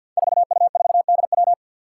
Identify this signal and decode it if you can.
Morse code, heard as a keyed tone, about 47 wpm, 700 Hz. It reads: 4U4DW
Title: Five Challenging Letter-Number Combinations 50wpm